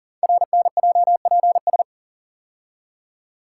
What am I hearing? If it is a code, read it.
Morse code, heard as a keyed tone, about 38 wpm, 690 Hz. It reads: RNJPS
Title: Code Groups - 5 Letters - 40wpm